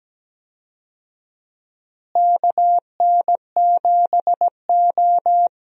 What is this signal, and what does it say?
Morse code, heard as a keyed tone, about 17 wpm, 700 Hz. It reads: KN7O
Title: USA Call Signs - 4 Characters - Repeated 2x - 17wpm